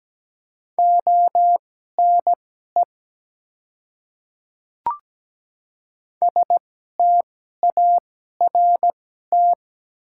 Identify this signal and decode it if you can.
Morse code, heard as a keyed tone, about 17 wpm, spaced as for 12 wpm, 690 Hz. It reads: ONE START
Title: 2x Character Spacing - Top 300 Words - 17wpm